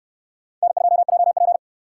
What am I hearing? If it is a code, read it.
Morse code, heard as a keyed tone, about 65 wpm, 700 Hz. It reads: N2JP